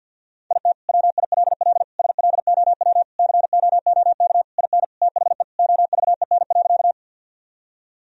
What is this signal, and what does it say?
Morse code, heard as a keyed tone, about 50 wpm, 700 Hz. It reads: IT WILL SLOW COOK IN THE OVEN.
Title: Sentences from Top 1000 Words - Review Part 4 - 50wpm